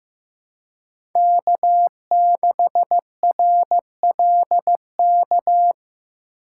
Morse code, heard as a keyed tone, about 15 wpm, 700 Hz. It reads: K6RLK
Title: USA Call Signs - 5 Characters - 2x Character Spacing - 15wpm